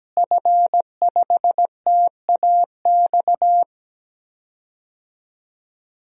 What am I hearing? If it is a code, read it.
Morse code, heard as a keyed tone, about 17 wpm, 700 Hz. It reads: F5TAX
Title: DXCC-100 Call Signs - 5 Characters - Repeated 2x - 17wpm